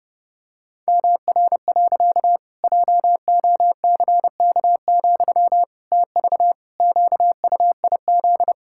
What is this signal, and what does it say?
Morse code, heard as a keyed tone, about 30 wpm, 700 Hz. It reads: MR. JOCK, TV QUIZ
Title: Pangram Challenge v2 - 30wpm